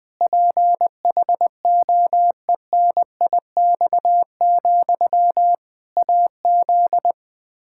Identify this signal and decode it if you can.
Morse code, heard as a keyed tone, about 20 wpm, 700 Hz. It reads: PHOENIX, AZ
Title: Practice QSO - Exchange 1 - Beginning - 20wpm